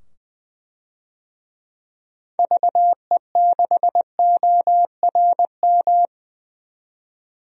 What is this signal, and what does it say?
Morse code, heard as a keyed tone, about 20 wpm, 700 Hz. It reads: VE6ORM
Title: DXCC-100 Call Signs - 6 Characters - 20wpm